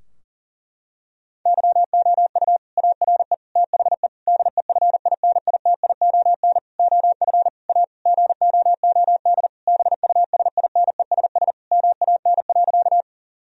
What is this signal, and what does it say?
Morse code, heard as a keyed tone, about 40 wpm, 700 Hz. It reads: YOU ARE THE DEFINITION OF A GOOD BUSINESS MAN.
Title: Sentences from Top 1000 Words - Review Part 4 - 40wpm